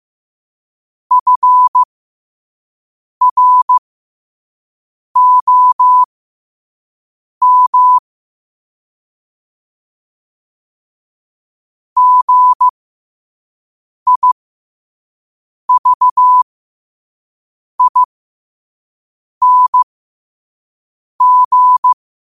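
Morse code, heard as a keyed tone, about 15 wpm, spaced as for 5 wpm, 1000 Hz. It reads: FROM GIVING
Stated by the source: Quote-of-the-day Morse code practice at 5 wpm.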